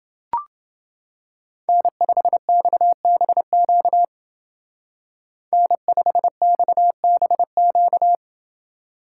Morse code, heard as a keyed tone, about 28 wpm, 700 Hz. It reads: N5XBQ N5XBQ